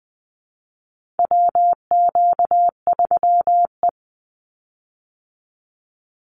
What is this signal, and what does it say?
Morse code, heard as a keyed tone, about 20 wpm, 700 Hz. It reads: WQ3E
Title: Call Signs Repeated 2x - 20wpm